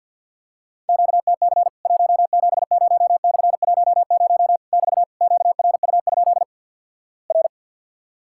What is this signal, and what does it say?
Morse code, heard as a keyed tone, about 50 wpm, 700 Hz. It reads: QTC 170/10 <BT> QRU? R
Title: Worked All Europe DX Contest - QTC Exchange - 50wpm